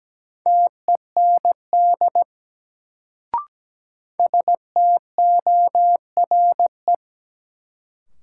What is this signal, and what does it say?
Morse code, heard as a keyed tone, about 17 wpm, 700 Hz. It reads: TEND STORE
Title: Sentences from Top 1000 Words - Review Part 5 - New Word Drill - 17wpm